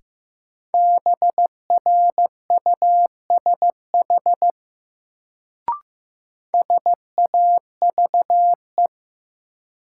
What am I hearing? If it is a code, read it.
Morse code, heard as a keyed tone, about 15 wpm, 700 Hz. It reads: BRUSH SAVE
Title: Sentences from Top 1100 Words - New Word Drill - 15wpm